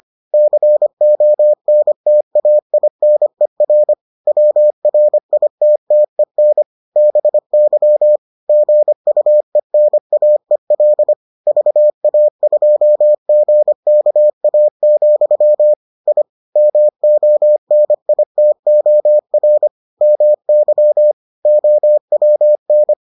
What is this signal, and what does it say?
Morse code, heard as a keyed tone, about 25 wpm, 600 Hz. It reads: CONTAINER WRITTEN BY GUENAEL VA2GKA, I MONITOR MY OWN